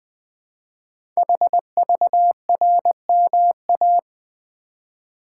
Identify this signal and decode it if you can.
Morse code, heard as a keyed tone, about 20 wpm, 700 Hz. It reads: HVRMA
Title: Code Groups - 5 Letters - 20wpm